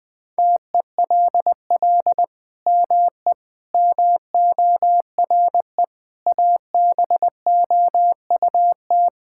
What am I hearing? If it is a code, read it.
Morse code, heard as a keyed tone, about 19 wpm, 700 Hz. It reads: TELL ME MORE ABOUT
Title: Sentences from Top 1000 Words - Review Part 6 - 20wpm